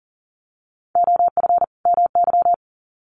Morse code, heard as a keyed tone, about 39 wpm, 700 Hz. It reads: OF MY
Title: Sets of 2 Words 40wpm